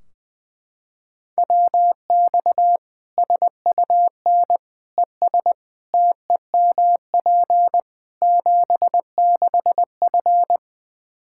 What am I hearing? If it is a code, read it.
Morse code, heard as a keyed tone, about 20 wpm, 700 Hz. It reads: WX SUN ES TEMP 76F
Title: K1USN SST QSO Practice 20wpm